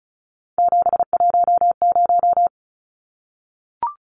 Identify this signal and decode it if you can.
Morse code, heard as a keyed tone, about 34 wpm, 700 Hz. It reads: 710
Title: Numbers - 3-Digit 35wpm